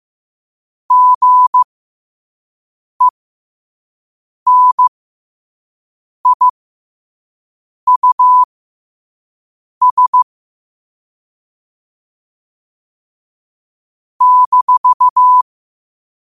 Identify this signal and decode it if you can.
Morse code, heard as a keyed tone, about 15 wpm, spaced as for 5 wpm, 1000 Hz. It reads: GENIUS -
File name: qotdfiles / 5wpm